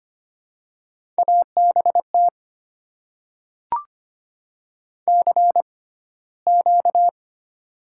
Morse code, heard as a keyed tone, about 25 wpm, spaced as for 15 wpm, 700 Hz. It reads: ABT CQ
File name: Top 90 QSO Elements - 6x Character Spacing - 25wpm